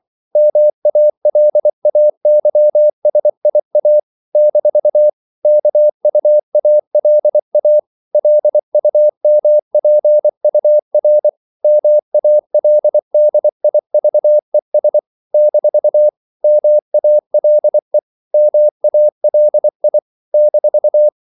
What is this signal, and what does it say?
Morse code, heard as a keyed tone, about 24 wpm, 600 Hz. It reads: MALAYSIA - KUALA LUMPUR MALDIVES - MALE MALI -
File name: CW / Practice / NationalCapitals24